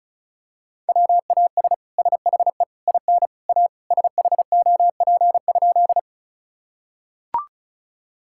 Morse code, heard as a keyed tone, about 36 wpm, 690 Hz. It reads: WAS SHE IN A SHOP?
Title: Sentences from Top 1600 Words 35wpm